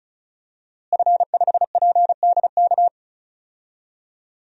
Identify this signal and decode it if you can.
Morse code, heard as a keyed tone, about 34 wpm, 700 Hz. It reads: F5PDK
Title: DXCC-100 Call Signs - 5 Characters - Repeated 3x - 35wpm